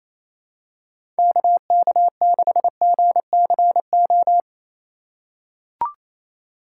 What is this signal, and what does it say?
Morse code, heard as a keyed tone, about 27 wpm, 700 Hz. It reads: KK6GCO
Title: USA Call Signs - 4x Character Spacing - 28wpm